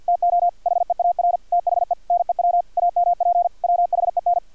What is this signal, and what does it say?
Morse code, heard as a keyed tone, about 50 wpm, 700 Hz. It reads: TO HEAR THE NEW AMP WHEN
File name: archived / CRSnet